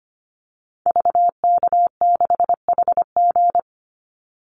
Morse code, heard as a keyed tone, about 25 wpm, 700 Hz. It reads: VK6HG
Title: Call Signs - 25wpm